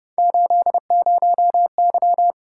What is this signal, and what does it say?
Morse code, heard as a keyed tone, about 29 wpm, 700 Hz. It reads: 80Y
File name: Speed-Racing - Three Letter-Number Combinations - 25wpm